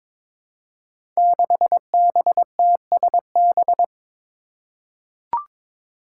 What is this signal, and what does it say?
Morse code, heard as a keyed tone, about 22 wpm, 700 Hz. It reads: 6BTSB
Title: Five Challenging Letter-Number Combinations 22wpm